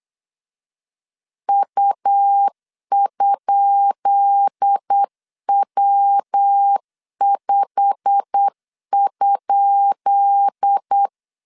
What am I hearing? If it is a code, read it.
Morse code, heard as a keyed tone, about 8 wpm, 800 Hz. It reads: U?W5?
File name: LEZIONI / 10 SIGNS / 10C DOT 8wpm 3649